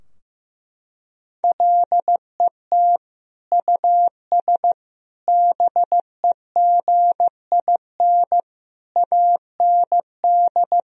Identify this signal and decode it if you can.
Morse code, heard as a keyed tone, about 15 wpm, 700 Hz. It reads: LET US BEGIN AND
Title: Sentences from Top 1000 Words - Review Part 3 - 15wpm